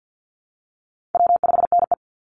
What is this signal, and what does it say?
Morse code, heard as a keyed tone, about 50 wpm, 700 Hz. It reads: W5NE
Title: Easy Call Signs Repeated 2x - 50wpm